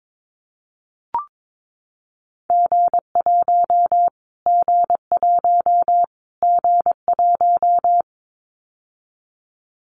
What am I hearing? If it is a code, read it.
Morse code, heard as a keyed tone, about 22 wpm, 700 Hz. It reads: G1 G1 G1
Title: Callsign Prefixes for Top 25 DXCC - Repeated 3x - 22wpm